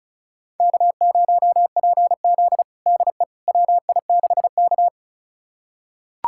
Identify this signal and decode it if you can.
Morse code, heard as a keyed tone, about 36 wpm, 700 Hz. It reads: K0PZ DE WI6K
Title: USA Callsign DE Callsign - 4 Characters - 35wpm